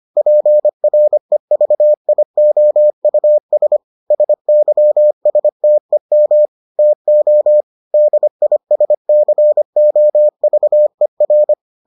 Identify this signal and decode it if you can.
Morse code, heard as a keyed tone, about 25 wpm, 600 Hz. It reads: PREVIOUS SYSTEM TO DISCOVER